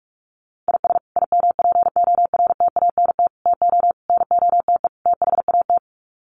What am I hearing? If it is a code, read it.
Morse code, heard as a keyed tone, about 44 wpm, 700 Hz. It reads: IS IMPORTANT TO NOTE THAT
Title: Sets of 5 Words 45wpm